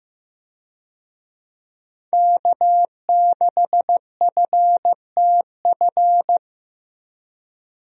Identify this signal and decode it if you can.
Morse code, heard as a keyed tone, about 15 wpm, 700 Hz. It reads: K6FTF
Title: USA Call Signs - 5 Characters - Repeated 3x - 15wpm